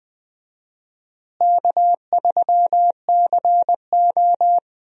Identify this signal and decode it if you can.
Morse code, heard as a keyed tone, about 20 wpm, 700 Hz. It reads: K3CO
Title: USA Call Signs - 4 Characters - Repeated 2x - 20wpm